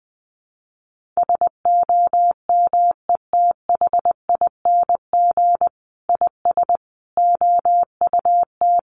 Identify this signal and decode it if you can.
Morse code, heard as a keyed tone, about 20 wpm, 700 Hz. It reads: SOMETHING IS OUT
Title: Sentences from Top 1400 Words 20wpm